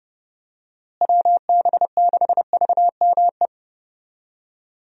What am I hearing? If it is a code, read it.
Morse code, heard as a keyed tone, about 30 wpm, 700 Hz. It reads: WB6VME
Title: USA Call Signs - 6 Characters - Repeated 2x - 30wpm